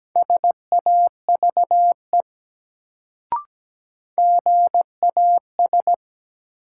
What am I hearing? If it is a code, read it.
Morse code, heard as a keyed tone, about 17 wpm, 700 Hz. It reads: SAVE GAS
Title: Sentences from Top 1100 Words - New Word Drill - 17wpm